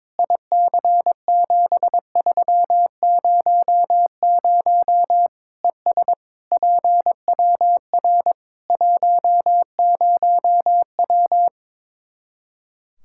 Morse code, heard as a keyed tone, about 22 wpm, 700 Hz. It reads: IC7300 ES PWR 10W
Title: K1USN SST QSO Practice 22wpm